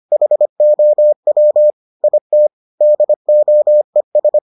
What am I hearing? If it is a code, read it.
Morse code, heard as a keyed tone, about 25 wpm, 600 Hz. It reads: HOW IT DOES